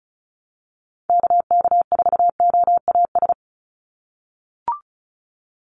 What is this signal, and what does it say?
Morse code, heard as a keyed tone, about 34 wpm, 700 Hz. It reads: KK4OAS
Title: USA Call Signs 35wpm